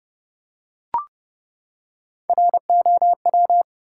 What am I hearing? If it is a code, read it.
Morse code, heard as a keyed tone, about 32 wpm, 700 Hz. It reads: ROW
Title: Three Letter Words - 30wpm